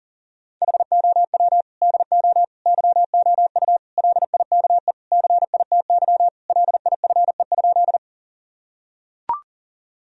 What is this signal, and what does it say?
Morse code, heard as a keyed tone, about 40 wpm, 700 Hz. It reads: HOW DO YOU LIKE CITY LIFE?
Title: Sentences from Top 1000 Words - Review Part 1 - 40wpm